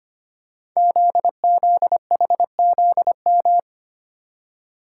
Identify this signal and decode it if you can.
Morse code, heard as a keyed tone, about 25 wpm, 700 Hz. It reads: ZZHZM
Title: Code Groups - 5 Letters - 25wpm